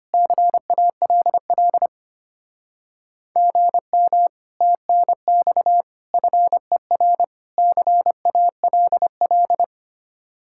Morse code, heard as a keyed tone, about 31 wpm, 700 Hz. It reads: CALL GM TNX FER CALL
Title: Speed-Racing Common QSO Phrases Encore - 25wpm